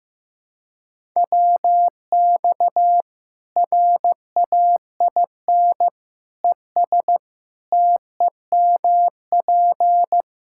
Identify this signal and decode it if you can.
Morse code, heard as a keyed tone, about 15 wpm, 700 Hz. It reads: WX RAIN ES TEMP
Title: K1USN SST QSO Practice 15wpm